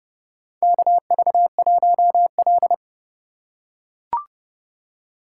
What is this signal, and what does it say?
Morse code, heard as a keyed tone, about 32 wpm, 700 Hz. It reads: KV1L
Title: USA Call Signs - 4 Characters - 2x Character Spacing - 30wpm